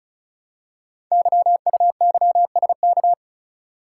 Morse code, heard as a keyed tone, about 36 wpm, 690 Hz. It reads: YUYSK
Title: Code Groups - 5 Letters - 35wpm